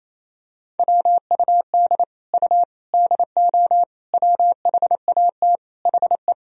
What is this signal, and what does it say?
Morse code, heard as a keyed tone, about 28 wpm, 700 Hz. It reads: WUD U DO WHAT HE
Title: Sentences from Top 100 Words and Abbreviations 28wpm